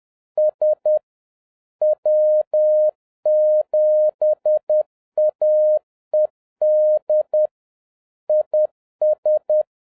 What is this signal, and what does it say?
Morse code, heard as a keyed tone, about 10 wpm, 610 Hz. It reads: S W7AED IS